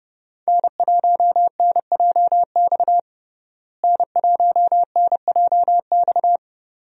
Morse code, heard as a keyed tone, about 30 wpm, 700 Hz. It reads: N1NJX N1NJX
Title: USA Call Signs - 5 Characters - Repeated 2x - 30wpm